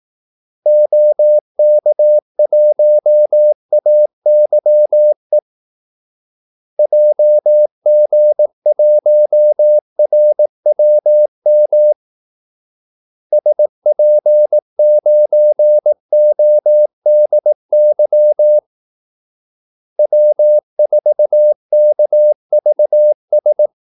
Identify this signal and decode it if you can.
Morse code, heard as a keyed tone, about 18 wpm, 600 Hz. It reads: OK1AYE JG1RWM SP9ODY W4KVS
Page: Kaldesignaler 18 wpm